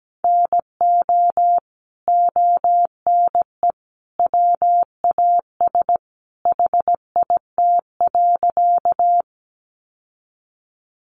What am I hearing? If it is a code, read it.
Morse code, heard as a keyed tone, about 17 wpm, 700 Hz. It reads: NO ONE WAS HIT.
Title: Sentences from Top 1200 Words 17wpm